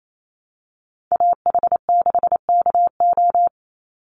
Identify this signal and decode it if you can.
Morse code, heard as a keyed tone, about 28 wpm, 700 Hz. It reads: AH6KO
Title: Call Signs Repeated 3x - 28wpm